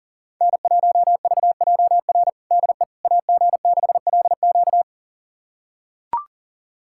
Morse code, heard as a keyed tone, about 40 wpm, 700 Hz. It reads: N1VJR DE AG6LQ
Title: USA Callsign DE Callsign - 5 Characters - 40wpm